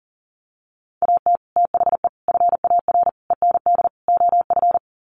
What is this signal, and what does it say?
Morse code, heard as a keyed tone, about 40 wpm, 700 Hz. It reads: AT THE FAR END OF
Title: Sets of 5 Words 40wpm